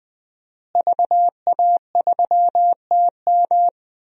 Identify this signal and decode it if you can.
Morse code, heard as a keyed tone, about 21 wpm, 700 Hz. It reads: VA3TM
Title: DXCC-100 Call Signs - 5 Characters - Repeated 2x - 20wpm